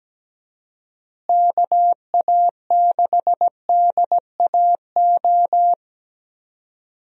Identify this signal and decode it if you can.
Morse code, heard as a keyed tone, about 17 wpm, 700 Hz. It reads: KA6DAO
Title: USA Call Signs - 6 Characters - Repeated 2x - 17wpm